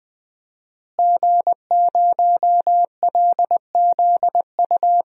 Morse code, heard as a keyed tone, about 20 wpm, 700 Hz. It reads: G0LZU
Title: DXCC-100 Call Signs - 5 Characters - Repeated 3x - 20wpm